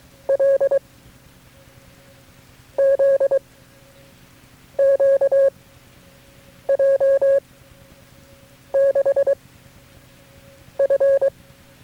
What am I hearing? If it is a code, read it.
Morse code, heard as a keyed tone, about 23 wpm, 550 Hz. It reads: L Z Q J 6 F
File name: FreeFiles / CW-practice / KB6MT-tape3-side1